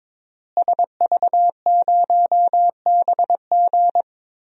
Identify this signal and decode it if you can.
Morse code, heard as a keyed tone, about 22 wpm, 700 Hz. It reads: SV0BG
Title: DXCC-100 Call Signs - 5 Characters - Repeated 3x - 22wpm